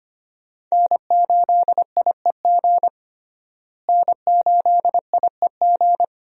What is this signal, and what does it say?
Morse code, heard as a keyed tone, about 25 wpm, 700 Hz. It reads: N8IEG N8IEG
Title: USA Call Signs - 5 Characters - Repeated 2x - 25wpm